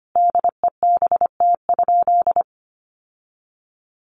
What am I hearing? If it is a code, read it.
Morse code, heard as a keyed tone, about 25 wpm, 700 Hz. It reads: DEBT?
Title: Sentences from Top 1300 Words 25wpm